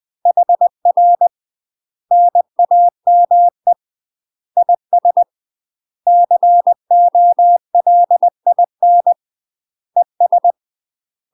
Morse code, heard as a keyed tone, about 20 wpm, 700 Hz. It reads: HR NAME IS COLIN ES
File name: UK MASTER QSOs TRAINING / 20WPM / UK MASTER QSOs 0069